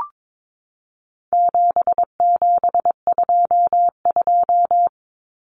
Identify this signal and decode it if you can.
Morse code, heard as a keyed tone, about 22 wpm, 700 Hz. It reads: 7722
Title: Numbers - 4-Digit 22wpm